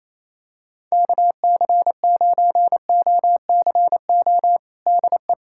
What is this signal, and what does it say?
Morse code, heard as a keyed tone, about 28 wpm, 700 Hz. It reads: KC9OCO DE
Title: USA Callsign DE Callsign 28wpm